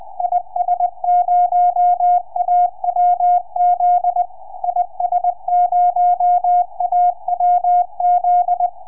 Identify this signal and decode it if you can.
Morse code, heard as a keyed tone, about 20 wpm, 710 Hz. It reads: IS0AWZ IS0AWZ
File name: is0awz 4m